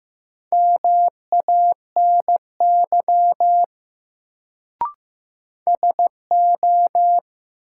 Morse code, heard as a keyed tone, about 15 wpm, 700 Hz. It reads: MANY SO
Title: Sentences from Top 100 Words - New Word Drill - 15wpm